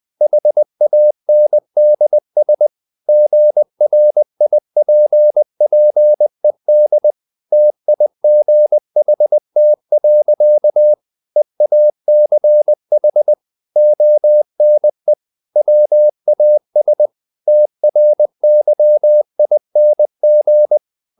Morse code, heard as a keyed tone, about 20 wpm, 600 Hz. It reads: HANDS GRIPPED TIGHT. EACH ONE WAS TRYING
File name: the old man and the sea / the old man and the sea part 1 20wpm